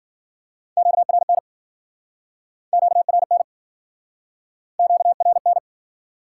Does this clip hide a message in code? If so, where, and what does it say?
Yes: Morse code from 0.8 to 5.6 s, heard as a keyed tone, about 57 wpm, 700 Hz. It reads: QRN QRN QRN